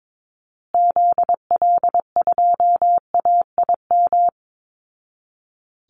Morse code, heard as a keyed tone, about 22 wpm, 700 Hz. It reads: ZL2AIM
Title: Call Signs Repeated 2x - 22wpm